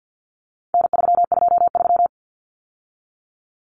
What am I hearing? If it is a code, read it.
Morse code, heard as a keyed tone, about 50 wpm, 700 Hz. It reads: N323
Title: Common Cut Numbers - 4-Digit 50wpm